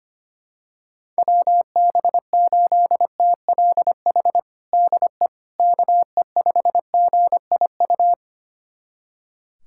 Morse code, heard as a keyed tone, about 25 wpm, 700 Hz. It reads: WB8TLH DE KE5GIU
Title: USA Callsign DE Callsign - 6 Characters - 25wpm